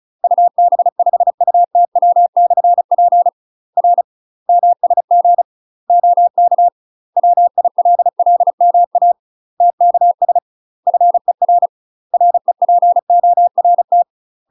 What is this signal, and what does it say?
Morse code, heard as a keyed tone, about 35 wpm, 700 Hz. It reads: UB5UTW/P R MSG OK WILLMA TKS FER REPORT
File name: UK MASTER QSOs TRAINING / 35WPM / UK MASTER QSOs 0034